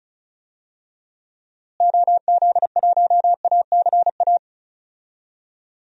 Morse code, heard as a keyed tone, about 34 wpm, 700 Hz. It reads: OZ1ACA
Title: DXCC-100 Call Signs - 6 Characters - Repeated 3x - 35wpm